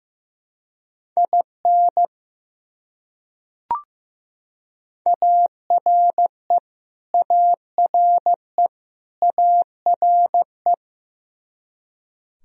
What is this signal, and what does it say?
Morse code, heard as a keyed tone, about 15 wpm, 700 Hz. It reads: IN ARE ARE ARE
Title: Top 100 Words Repeated 15wpm